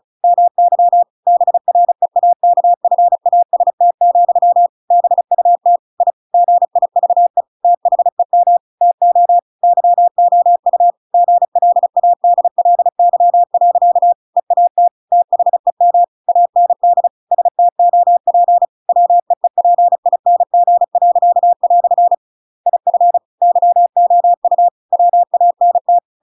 Morse code, heard as a keyed tone, about 36 wpm, 700 Hz. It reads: MY BREAKFAST, BUT I GIVE THEM TO YOU GLADLY. EAT THEM AND STOP WEEPING." IF YOU WANT
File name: Pinocchio / Pinocchio ch7 35 35 ik1hge